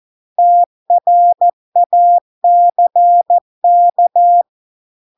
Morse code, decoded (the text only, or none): TRACK